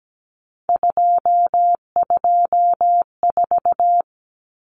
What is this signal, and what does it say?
Morse code, heard as a keyed tone, about 17 wpm, 700 Hz. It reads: 224
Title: Common Cut Numbers - 3-Digit 17wpm